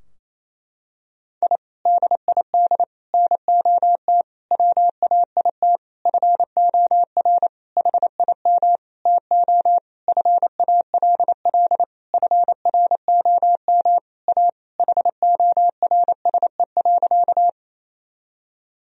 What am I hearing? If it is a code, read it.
Morse code, heard as a keyed tone, about 28 wpm, 700 Hz. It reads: I DID NOT WAIT FOR HIM TO FALL FROM A HORSE.
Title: Sentences from Top 1000 Words - Review Part 2 - 28wpm